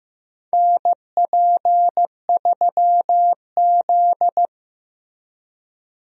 Morse code, heard as a keyed tone, about 15 wpm, 700 Hz. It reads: NP3Z
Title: USA Call Signs - 4 Characters - Repeated 3x - 15wpm